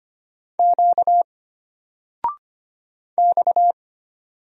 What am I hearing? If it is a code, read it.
Morse code, heard as a keyed tone, about 25 wpm, 700 Hz. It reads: Q X
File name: Single-Letter-Number-25wpm